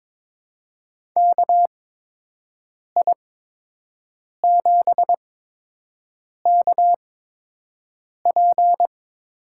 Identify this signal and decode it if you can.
Morse code, heard as a keyed tone, about 22 wpm, spaced as for 6 wpm, 700 Hz. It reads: KI7KP
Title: USA Call Signs - 6 Characters - 8x Character Spacing - 22wpm